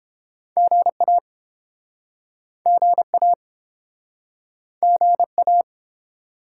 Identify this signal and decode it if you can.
Morse code, heard as a keyed tone, about 30 wpm, 700 Hz. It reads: GA GA GA